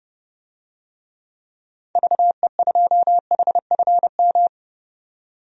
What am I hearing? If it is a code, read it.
Morse code, heard as a keyed tone, about 32 wpm, 700 Hz. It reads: VE2HFM
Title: DXCC-100 Call Signs - 6 Characters - 30wpm